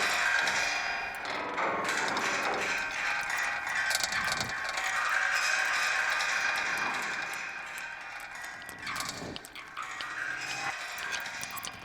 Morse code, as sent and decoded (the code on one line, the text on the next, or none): none